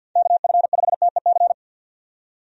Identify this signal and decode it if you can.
Morse code, heard as a keyed tone, about 51 wpm, 700 Hz. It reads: KF5TEC
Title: USA Call Signs - 6 Characters - 50wpm